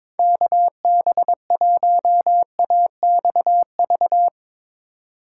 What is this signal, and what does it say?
Morse code, heard as a keyed tone, about 22 wpm, 700 Hz. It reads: KB1AXV
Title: Speed Racing - USA 2x3 Call Signs 22wpm